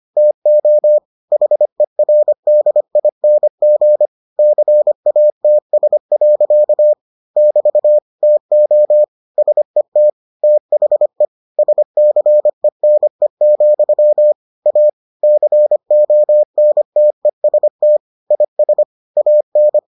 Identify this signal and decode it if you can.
Morse code, heard as a keyed tone, about 25 wpm, 600 Hz. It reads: TO HERDING CATS. <BT> TO SET THE SCENE, A CONTEST IS AN